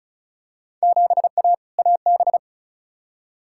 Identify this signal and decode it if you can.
Morse code, heard as a keyed tone, about 36 wpm, 700 Hz. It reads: 7A AB
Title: Field Day Exchanges Encore - 35wpm